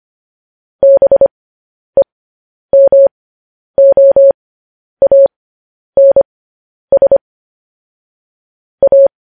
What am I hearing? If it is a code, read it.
Morse code, heard as a keyed tone, about 25 wpm, spaced as for 10 wpm, 550 Hz. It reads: BEMOANS A
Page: news-headlines-in-morse-code-at-10-wpm